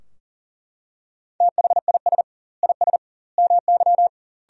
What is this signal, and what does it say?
Morse code, heard as a keyed tone, about 40 wpm, 700 Hz. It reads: THIS IS MY
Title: 2x Word-Spacing - Sentences from Top 300 Words - 40wpm